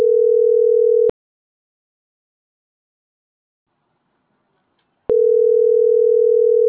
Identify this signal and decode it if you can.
Ringback tone (North America), 440 + 480 Hz.